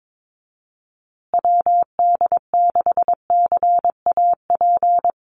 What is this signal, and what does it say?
Morse code, heard as a keyed tone, about 22 wpm, 700 Hz. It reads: WD6CAP
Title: USA Call Signs Repeated 2x 22wpm